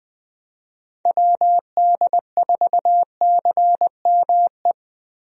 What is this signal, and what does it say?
Morse code, heard as a keyed tone, about 21 wpm, 700 Hz. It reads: WD4CME
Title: USA Call Signs - 6 Characters - Repeated 3x - 20wpm